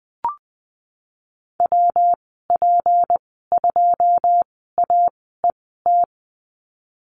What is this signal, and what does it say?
Morse code, heard as a keyed tone, about 20 wpm, spaced as for 15 wpm, 700 Hz. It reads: WP2AET
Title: 186 Lesson 30g - Letter b - Call Signs 20wpm